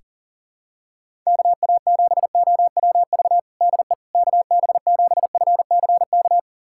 Morse code, heard as a keyed tone, about 40 wpm, 700 Hz. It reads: KA7OWV DE KB7FCK
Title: USA Callsign DE Callsign - 6 Characters - 40wpm